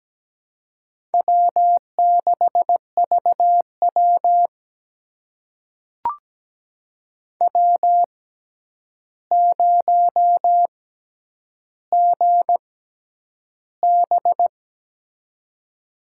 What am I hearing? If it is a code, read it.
Morse code, heard as a keyed tone, about 17 wpm, spaced as for 9 wpm, 700 Hz. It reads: W6VW W0GB
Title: USA Call Signs - 4 Characters - 6x Character Spacing - 17wpm